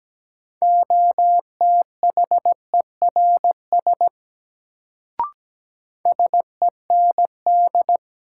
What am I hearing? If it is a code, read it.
Morse code, heard as a keyed tone, about 17 wpm, 700 Hz. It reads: OTHERS SEND
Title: Sentences from Top 1500 Words - New Word Drill Alt - 17wpm